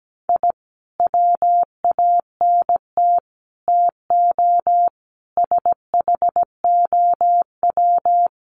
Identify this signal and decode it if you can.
Morse code, heard as a keyed tone, about 17 wpm, 700 Hz. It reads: I WANT TO SHOW
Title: Sets of 5 Words 17wpm